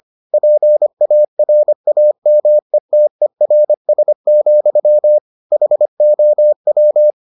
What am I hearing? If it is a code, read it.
Morse code, heard as a keyed tone, about 25 wpm, 600 Hz. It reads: PARAMETERS, HOW